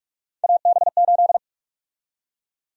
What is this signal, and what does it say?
Morse code, heard as a keyed tone, about 46 wpm, 700 Hz. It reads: AB8
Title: USA Call Sign Prefixes - 45wpm